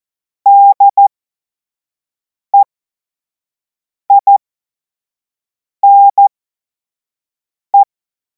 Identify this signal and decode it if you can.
Morse code, heard as a keyed tone, about 14 wpm, spaced as for 5 wpm, 800 Hz. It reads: DEINE
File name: Morsekurs / Witze / Witz 17. März 2026 25 F70